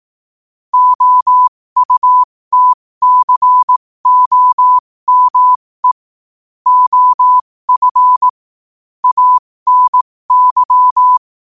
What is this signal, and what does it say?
Morse code, heard as a keyed tone, about 18 wpm, spaced as for 15 wpm, 1000 Hz. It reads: OUTCOME OF ANY
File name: qotdfiles / 15wpm